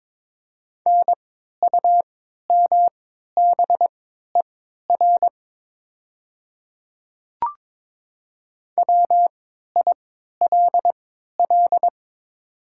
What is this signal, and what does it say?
Morse code, heard as a keyed tone, about 22 wpm, spaced as for 12 wpm, 700 Hz. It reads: NUMBER WILL
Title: 3x Character Spacing - Top 100 Words - 22wpm